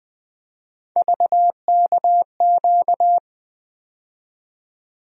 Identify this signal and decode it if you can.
Morse code, heard as a keyed tone, about 20 wpm, 700 Hz. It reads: VKQ
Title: USA Call Sign Postfixes - 3 Letters 20wpm